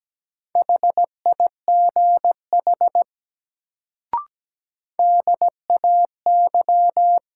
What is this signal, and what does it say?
Morse code, heard as a keyed tone, about 17 wpm, 700 Hz. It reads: HIGH DAY
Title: Sentences from Top 200 Words - New Word Drill - 17wpm